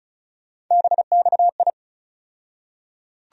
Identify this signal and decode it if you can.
Morse code, heard as a keyed tone, about 34 wpm, 700 Hz. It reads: BXI
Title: USA Call Sign Postfixes - 3 Letters 35wpm